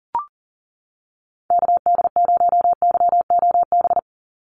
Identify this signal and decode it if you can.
Morse code, heard as a keyed tone, about 40 wpm, 700 Hz. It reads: KD0YOB
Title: NA SOTA Activator Callsigns 40wpm